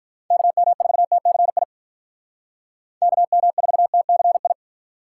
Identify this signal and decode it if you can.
Morse code, heard as a keyed tone, about 43 wpm, 700 Hz. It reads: KM4TCI KM4TCI
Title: Speed Racing - USA 2x3 Call Signs 35wpm